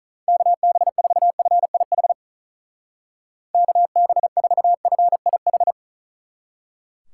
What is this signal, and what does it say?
Morse code, heard as a keyed tone, about 45 wpm, 700 Hz. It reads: KB4FIH KB4FIH